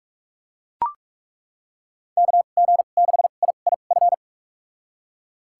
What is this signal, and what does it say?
Morse code, heard as a keyed tone, about 41 wpm, spaced as for 31 wpm, 700 Hz. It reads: KG6IIF
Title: USA Call Signs - 6 Characters - 2x Character Spacing - 45wpm